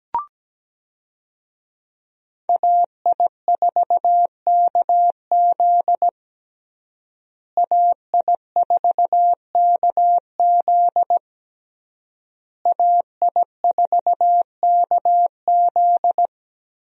Morse code, heard as a keyed tone, about 17 wpm, 700 Hz. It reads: AI4KZ AI4KZ AI4KZ
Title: USA Call Signs - 5 Characters - Repeated 3x - 17wpm